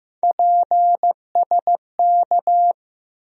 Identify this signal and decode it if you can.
Morse code, heard as a keyed tone, about 15 wpm, 700 Hz. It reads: PSK
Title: USA Call Sign Postfixes - 3 Letters 15wpm